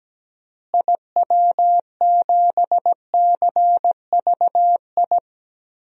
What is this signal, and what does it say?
Morse code, heard as a keyed tone, about 17 wpm, 700 Hz. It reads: IW7CVI
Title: DXCC-100 Call Signs - 6 Characters - Repeated 3x - 17wpm